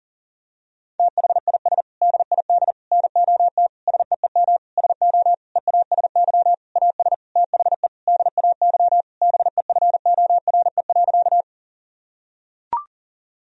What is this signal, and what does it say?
Morse code, heard as a keyed tone, about 37 wpm, 700 Hz. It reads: THIS DID NOT SEEM SO EASY AS THE DAY BEFORE.
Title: Sentences from Top 1000 Words - Review Part 1 - 40wpm